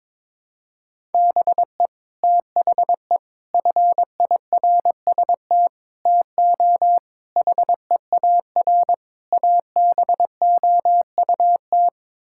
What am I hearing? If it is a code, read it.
Morse code, heard as a keyed tone, about 22 wpm, 700 Hz. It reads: BE THE FIRST TO HEAR ABOUT
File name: practice / 22wpm / Sets of 6 Words - 22wpm